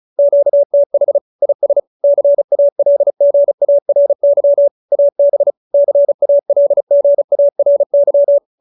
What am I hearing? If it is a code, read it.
Morse code, heard as a keyed tone, about 35 wpm, 570 Hz. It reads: QTH IS CALGARY AB CALGARY